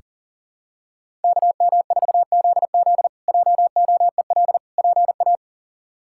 Morse code, heard as a keyed tone, about 37 wpm, 700 Hz. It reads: KM4ZZ JOEL PA
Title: K1USN SST QSO Practice - 40wpm